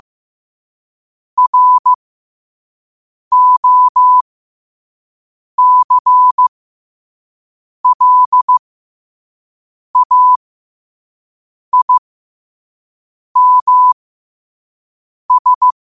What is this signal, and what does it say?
Morse code, heard as a keyed tone, about 15 wpm, spaced as for 5 wpm, 1000 Hz. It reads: ROCLAIMS
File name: qotdfiles / 5wpm